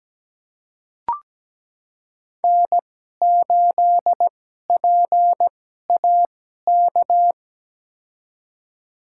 Morse code, heard as a keyed tone, about 17 wpm, spaced as for 12 wpm, 700 Hz. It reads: N8PAK
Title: USA Call Signs - 5 Characters - 2x Character Spacing - 17wpm